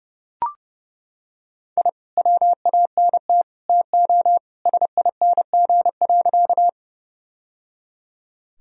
Morse code, heard as a keyed tone, about 30 wpm, 700 Hz. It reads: I WANT TO SING.
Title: Sentences from Top 1000 Words - Review Part 2 - 30wpm